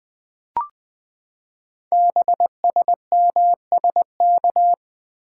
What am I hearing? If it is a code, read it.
Morse code, heard as a keyed tone, about 20 wpm, 700 Hz. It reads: BSMSK
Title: Code Groups - 5 Letters - 20wpm